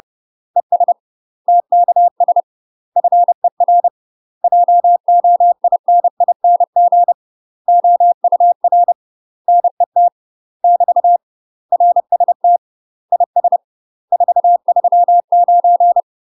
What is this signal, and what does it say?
Morse code, heard as a keyed tone, about 30 wpm, 700 Hz. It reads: ES TKS FER JOINING OUR NET <BT> RST IS 439
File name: UK MASTER QSOs TRAINING/30WPM